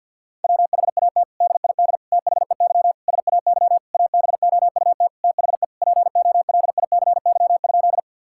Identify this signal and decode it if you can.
Morse code, heard as a keyed tone, about 51 wpm, 690 Hz. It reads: WHAT DID THEY SAY ABOUT THE POLICY?
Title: Sentences from Top 1000 Words - Review Part 5 - 50wpm